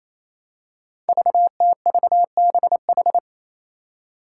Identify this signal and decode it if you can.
Morse code, heard as a keyed tone, about 28 wpm, 700 Hz. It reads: VTVBH
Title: Five Challenging Letter-Number Combinations 28wpm